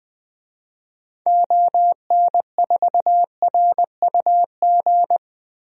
Morse code, heard as a keyed tone, about 20 wpm, 700 Hz. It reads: ON4RUG
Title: DXCC-100 Call Signs - 6 Characters - 20wpm